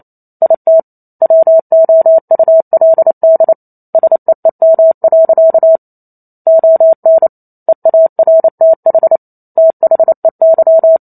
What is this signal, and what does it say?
Morse code, heard as a keyed tone, about 28 wpm, 650 Hz. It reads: IT WOULD SEEM. ON EARTH THEY